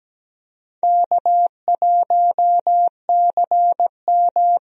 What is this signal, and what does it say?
Morse code, heard as a keyed tone, about 17 wpm, 700 Hz. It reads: K1CM
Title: USA Call Signs - 4 Characters - Repeated 2x - 17wpm